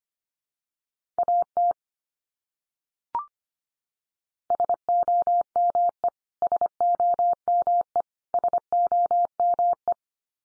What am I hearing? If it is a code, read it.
Morse code, heard as a keyed tone, about 25 wpm, 700 Hz. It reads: AT SOME SOME SOME
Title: Top 100 Words 25wpm x3